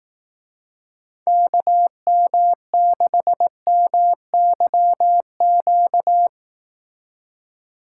Morse code, heard as a keyed tone, about 18 wpm, 700 Hz. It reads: KM6MYQ
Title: Speed Racing - USA 2x3 Call Signs 15wpm